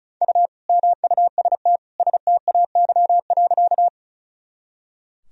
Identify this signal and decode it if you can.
Morse code, heard as a keyed tone, about 36 wpm, 700 Hz. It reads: U MUST STAY.
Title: Sentences from Top 500 Words and Abbreviations 35wpm